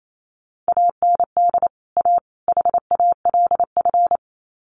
Morse code, heard as a keyed tone, about 28 wpm, 700 Hz. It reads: AND A HALF
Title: Sets of 3 Words 28wpm